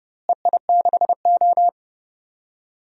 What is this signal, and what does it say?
Morse code, heard as a keyed tone, about 30 wpm, 700 Hz. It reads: EI6O
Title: DXCC-100 Call Signs - 4 Characters - Repeated 2x - 30wpm